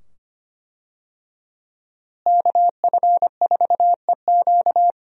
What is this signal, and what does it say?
Morse code, heard as a keyed tone, about 25 wpm, 700 Hz. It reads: KF4EQ
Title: USA Call Signs - 5 Characters - Repeated 2x - 25wpm